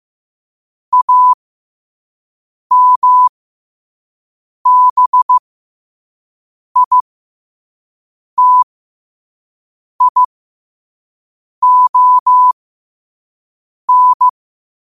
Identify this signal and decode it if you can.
Morse code, heard as a keyed tone, about 15 wpm, spaced as for 5 wpm, 1000 Hz. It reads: AMBITION
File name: qotdfiles / 5wpm